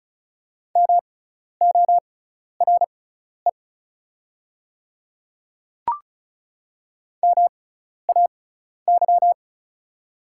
Morse code, heard as a keyed tone, about 34 wpm, spaced as for 12 wpm, 700 Hz. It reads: MORE MAY
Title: 6x Character Spacing - Top 100 Words - 35wpm